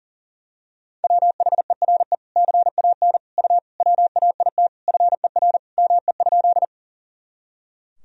Morse code, heard as a keyed tone, about 37 wpm, 700 Hz. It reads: WHERE CAN U WAIT FER ME?
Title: Sentences from Top 500 Words and Abbreviations 40wpm